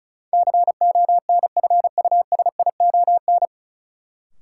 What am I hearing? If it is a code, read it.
Morse code, heard as a keyed tone, about 36 wpm, 700 Hz. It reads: CONFUSION
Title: Sentences from Top 1600 Words - New Word Drill - 35wpm